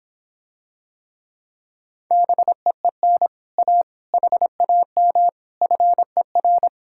Morse code, heard as a keyed tone, about 26 wpm, 700 Hz. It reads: BEEN A HAM FER